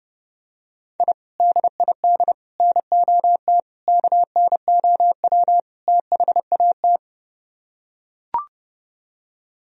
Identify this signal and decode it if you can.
Morse code, heard as a keyed tone, about 30 wpm, 700 Hz. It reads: I DID NOT KNOW THAT